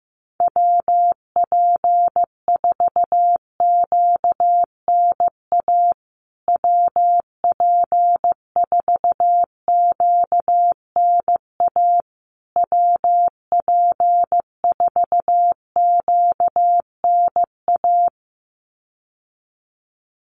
Morse code, heard as a keyed tone, about 14 wpm, 700 Hz. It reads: WP4QNA WP4QNA WP4QNA
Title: USA Call Signs Repeated 3x 15wpm